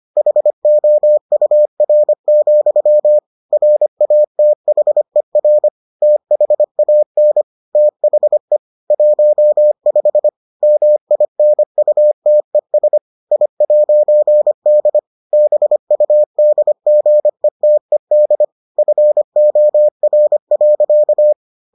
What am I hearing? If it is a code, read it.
Morse code, heard as a keyed tone, about 25 wpm, 600 Hz. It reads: HOUR, RATHER THAN THE 15 MINUTES I'D BUDGETED FOR.